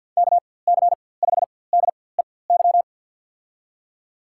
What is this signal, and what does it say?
Morse code, heard as a keyed tone, about 48 wpm, spaced as for 23 wpm, 700 Hz. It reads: KC5DEY
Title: USA Call Signs - 6 Characters - 4x Character Spacing - 50wpm